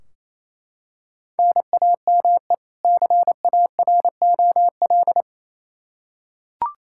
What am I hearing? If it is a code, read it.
Morse code, heard as a keyed tone, about 27 wpm, 700 Hz. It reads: NAME CAROL
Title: Practice QSO - Exchange 1 - Beginning - 28wpm